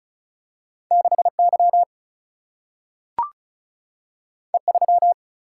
Morse code, heard as a keyed tone, about 36 wpm, 700 Hz. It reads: 6 Y E 3
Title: Single-Letter Single-Number 35wpm